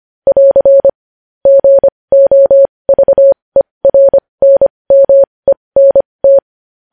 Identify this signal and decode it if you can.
Morse code, heard as a keyed tone, about 25 wpm, spaced as for 20 wpm, 550 Hz. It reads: <AR> GOVERNMENT
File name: podcast / News headlines-1x-25x20